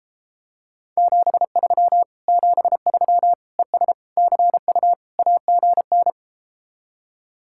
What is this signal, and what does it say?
Morse code, heard as a keyed tone, about 33 wpm, 700 Hz. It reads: 73 73 ES CU AGN